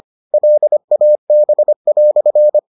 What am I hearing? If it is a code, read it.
Morse code, heard as a keyed tone, about 25 wpm, 600 Hz. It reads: LAB"